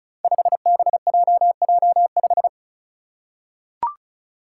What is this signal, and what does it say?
Morse code, heard as a keyed tone, about 34 wpm, 700 Hz. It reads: 5BJJ5
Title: Five Challenging Letter-Number Combinations 35wpm